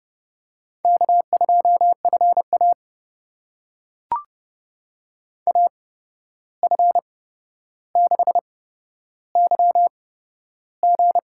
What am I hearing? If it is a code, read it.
Morse code, heard as a keyed tone, about 30 wpm, spaced as for 12 wpm, 700 Hz. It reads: K2FA AF6YG
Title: USA Call Signs - 8x Character Spacing - 30wpm